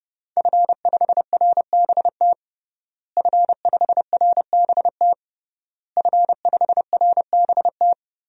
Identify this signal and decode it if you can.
Morse code, heard as a keyed tone, about 32 wpm, 700 Hz. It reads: F5RBT F5RBT F5RBT
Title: DXCC-100 Call Signs - 5 Characters - Repeated 3x - 30wpm